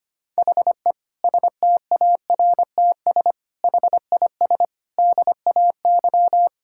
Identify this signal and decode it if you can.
Morse code, heard as a keyed tone, about 25 wpm, 700 Hz. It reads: HE STARTS HIS DAY
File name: practice / 25wpm / Difficult Sentences with Hints - 25wpm